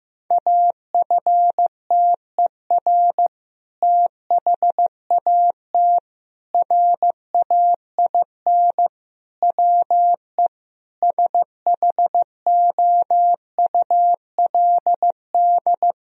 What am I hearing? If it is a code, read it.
Morse code, heard as a keyed tone, about 15 wpm, 700 Hz. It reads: AFTER THAT RAIN WE SHOULD
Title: Sentences from Top 1000 Words - Review Part 1 - 15wpm